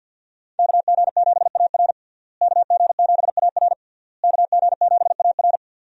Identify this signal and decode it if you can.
Morse code, heard as a keyed tone, about 52 wpm, 700 Hz. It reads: KG7AR KG7AR KG7AR
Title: USA Call Signs - 5 Characters - Repeated 3x - 50wpm